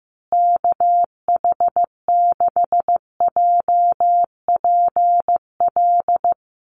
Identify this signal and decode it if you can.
Morse code, heard as a keyed tone, about 15 wpm, 700 Hz. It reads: KH6JPL
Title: USA Call Signs Repeated 3x 15wpm